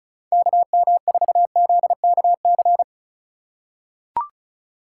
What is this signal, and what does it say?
Morse code, heard as a keyed tone, about 36 wpm, 700 Hz. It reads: KM4ZKC
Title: USA Call Signs - 6 Characters - 3x Character Spacing - 35wpm